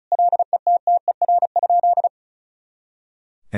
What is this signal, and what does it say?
Morse code, heard as a keyed tone, about 34 wpm, 700 Hz. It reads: LETTER?
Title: Sentences from Top 300 Words and Abbreviations 35wpm